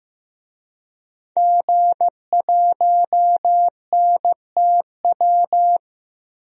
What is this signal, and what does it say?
Morse code, heard as a keyed tone, about 15 wpm, 700 Hz. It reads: G1NTW
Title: DXCC-100 Call Signs - 5 Characters - 15wpm